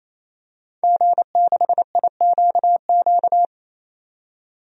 Morse code, heard as a keyed tone, about 28 wpm, 700 Hz. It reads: G6IQQ
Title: DXCC-100 Call Signs - 5 Characters - 28wpm